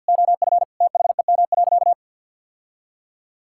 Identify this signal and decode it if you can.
Morse code, heard as a keyed tone, about 50 wpm, 700 Hz. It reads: OF THEM.
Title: Sentences from Top 1000 Words - Review Part 6 - 50wpm